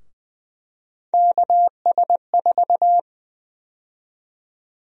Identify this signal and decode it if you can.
Morse code, heard as a keyed tone, about 20 wpm, 700 Hz. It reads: KS4
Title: USA Call Sign Prefixes - 20wpm